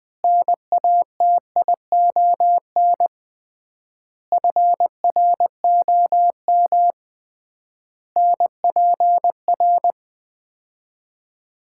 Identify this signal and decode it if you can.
Morse code, heard as a keyed tone, about 20 wpm, 690 Hz. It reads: NATION FROM NPR
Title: 3x Word-Spacing - Sets of 5 Words - 20wpm